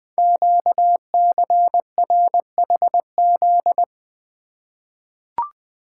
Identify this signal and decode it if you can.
Morse code, heard as a keyed tone, about 20 wpm, 700 Hz. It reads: QCRHZ
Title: Code Groups - 5 Letters - 20wpm